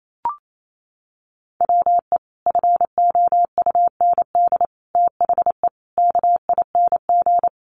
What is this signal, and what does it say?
Morse code, heard as a keyed tone, about 28 wpm, 700 Hz. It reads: WE FOUND THE KING
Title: Sentences from Top 1300 Words 28wpm